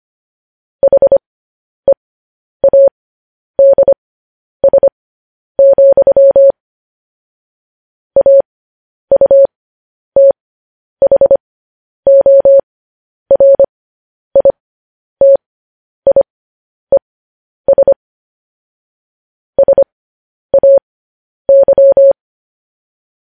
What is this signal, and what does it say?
Morse code, heard as a keyed tone, about 25 wpm, spaced as for 10 wpm, 550 Hz. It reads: HEADS, AUTHORITIES SAY